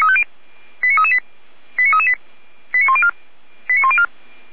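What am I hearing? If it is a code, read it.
Five-tone selective call (ZVEI), heard as quick runs of five tones, tones of about 70 ms. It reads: 89338 89338 89104 89104
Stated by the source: ZVEI five-tone call received on SDR.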